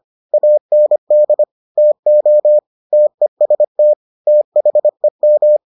Morse code, heard as a keyed tone, about 25 wpm, 600 Hz. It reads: AND TO TEST THEM